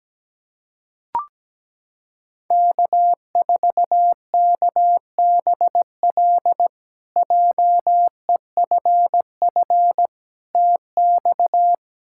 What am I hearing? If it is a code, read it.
Morse code, heard as a keyed tone, about 17 wpm, 700 Hz. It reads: K4KBL JEFF TX
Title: K1USN SST QSO Practice - 17wpm